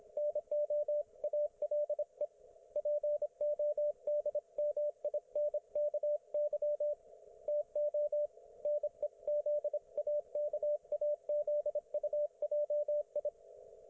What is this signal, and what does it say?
Morse code, heard as a keyed tone, about 25 wpm, 590 Hz, with radio noise. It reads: NO ALE PODMINKY TO NEZAKAZUJI
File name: Archiv / CUC194 / ok1fhi cul194